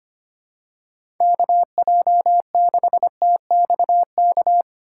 Morse code, heard as a keyed tone, about 25 wpm, 700 Hz. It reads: KJ6TXK
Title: USA Call Signs - 6 Characters - 6x Character Spacing - 25wpm